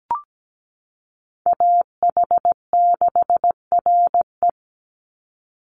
Morse code, heard as a keyed tone, about 17 wpm, 700 Hz. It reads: AH6RE
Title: Call Signs - 17wpm